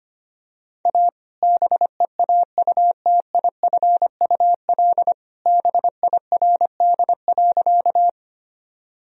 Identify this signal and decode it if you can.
Morse code, heard as a keyed tone, about 25 wpm, 700 Hz. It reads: A BEAUTIFUL BIRD.
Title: Sentences from Top 1000 Words - Review Part 2 - 25wpm